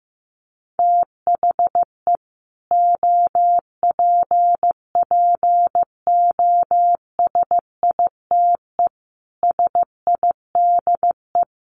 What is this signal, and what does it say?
Morse code, heard as a keyed tone, about 15 wpm, 700 Hz. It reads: THE OPPOSITE SIDE
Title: Sets of 5 Words Encore 15wpm